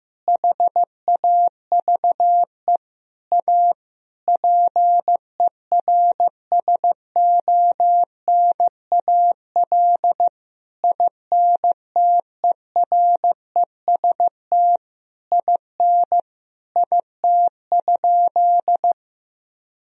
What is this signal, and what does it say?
Morse code, heard as a keyed tone, about 16 wpm, 700 Hz. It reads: HAVE A PERSONAL INTEREST IN IT?
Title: Sentences from Top 1000 Words - Review Part 4 - 15wpm